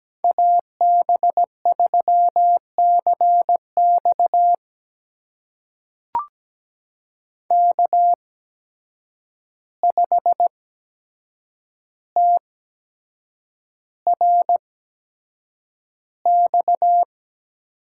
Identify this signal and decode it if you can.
Morse code, heard as a keyed tone, about 17 wpm, spaced as for 7 wpm, 700 Hz. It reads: AB3CX K5TRX
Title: USA Call Signs - 8x Character Spacing - 17wpm